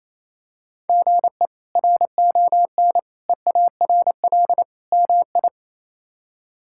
Morse code, heard as a keyed tone, about 28 wpm, 700 Hz. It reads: GE RON EARL MI
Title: K1USN SST QSO Practice - 28wpm